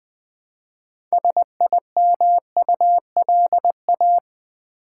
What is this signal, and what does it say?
Morse code, heard as a keyed tone, about 20 wpm, 700 Hz. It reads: SIMULA
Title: Build A Word - 2 Letters At A Time - 8-Letter Words - 20wpm